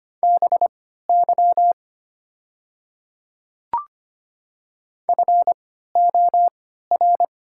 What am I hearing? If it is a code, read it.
Morse code, heard as a keyed tone, about 25 wpm, spaced as for 14 wpm, 700 Hz. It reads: BY FOR
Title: 3x Character Spacing - Top 100 Words - 25wpm